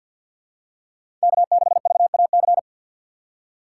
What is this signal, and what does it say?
Morse code, heard as a keyed tone, about 53 wpm, 700 Hz. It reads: K6VAC
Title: USA Call Signs - 4x Character Spacing - 50wpm